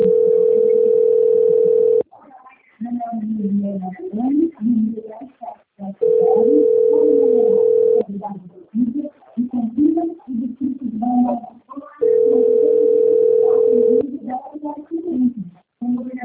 Ringback tone (North America), 440 + 480 Hz.
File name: out- - - - - .39927